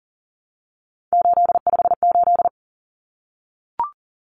Morse code, heard as a keyed tone, about 40 wpm, 700 Hz. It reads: 858
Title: Numbers - 3-Digit 40wpm